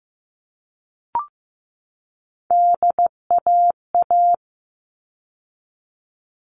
Morse code, heard as a keyed tone, about 15 wpm, 700 Hz. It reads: DAA
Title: All Cut Numbers - 3-Digit 15wpm